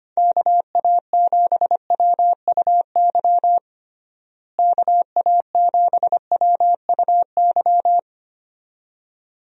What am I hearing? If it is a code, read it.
Morse code, heard as a keyed tone, about 25 wpm, 700 Hz. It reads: KA7WUY KA7WUY
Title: USA Call Signs - 6 Characters - Repeated 2x - 25wpm